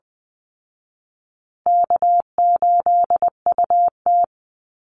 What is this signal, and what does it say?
Morse code, heard as a keyed tone, about 20 wpm, 700 Hz. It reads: K8UT
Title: Call Signs Repeated 3x - 20wpm